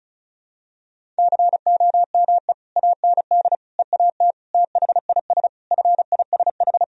Morse code, heard as a keyed tone, about 36 wpm, 700 Hz. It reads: COME AND EAT THIS FISH
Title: Sentences from Top 1000 Words - Review Part 1 - 35wpm